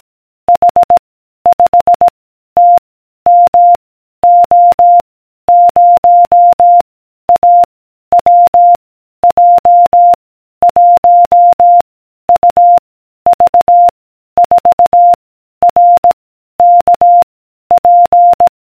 Morse code, heard as a keyed tone, about 18 wpm, 700 Hz. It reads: H 5 T M O 0 A W J 1 U V 4 R K P